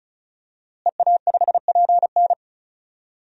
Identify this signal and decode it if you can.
Morse code, heard as a keyed tone, about 34 wpm, 700 Hz. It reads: EA5PN
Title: DXCC-100 Call Signs - 5 Characters - Repeated 2x - 35wpm